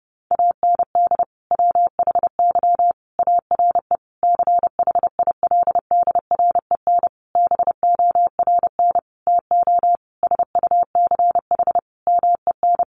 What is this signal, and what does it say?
Morse code, heard as a keyed tone, about 30 wpm, 700 Hz. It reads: AND WHY ARE CHILDREN BORN TO SUCH MEN
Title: Sentences from Top 1000 Words 30wpm